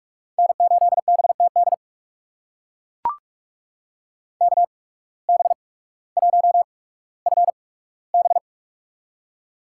Morse code, heard as a keyed tone, about 43 wpm, spaced as for 19 wpm, 700 Hz. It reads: N8BTD KB1FB
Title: USA Call Signs - 5 Characters - 8x Character Spacing - 45wpm